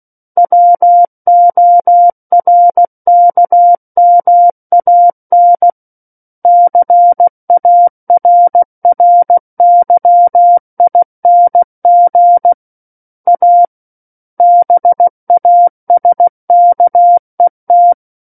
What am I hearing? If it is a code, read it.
Morse code, heard as a keyed tone, about 16 wpm, 700 Hz. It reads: WORKMAN CARRYING A BASKET
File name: Book One / 07-Chapter 7 - 16 WPM